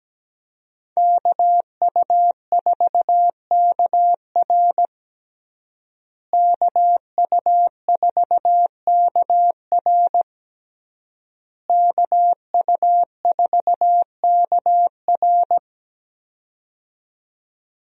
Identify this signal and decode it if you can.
Morse code, heard as a keyed tone, about 17 wpm, 700 Hz. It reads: KU4KR KU4KR KU4KR
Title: USA Call Signs - 5 Characters - Repeated 3x - 17wpm